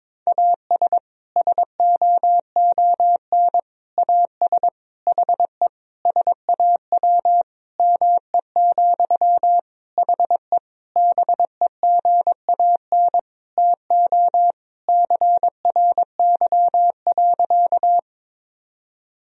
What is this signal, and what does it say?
Morse code, heard as a keyed tone, about 22 wpm, 700 Hz. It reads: AS SOON AS HE SAW ME, HE BEGAN TO CRY.
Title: Sentences from Top 1000 Words - Review Part 2 - 22wpm